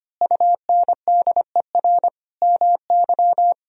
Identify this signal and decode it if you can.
Morse code, heard as a keyed tone, about 25 wpm, 700 Hz. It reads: UNDER MY
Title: Sentences from Top 1000 Words - Review Part 1 - 25wpm